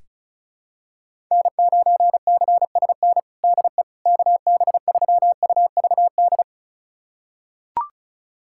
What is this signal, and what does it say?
Morse code, heard as a keyed tone, about 36 wpm, 700 Hz. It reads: N9CSN DE KB3UVD
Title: USA Callsign DE Callsign 35wpm